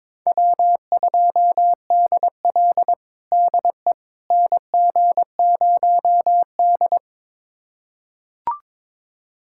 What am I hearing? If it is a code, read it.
Morse code, heard as a keyed tone, about 22 wpm, 700 Hz. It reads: W2DL DE NG0D
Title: USA Callsign DE Callsign - 4 Characters - 22wpm